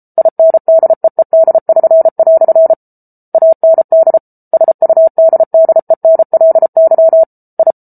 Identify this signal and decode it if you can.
Morse code, heard as a keyed tone, about 34 wpm, 650 Hz. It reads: INDEED<SN>" AND SUDDENLY I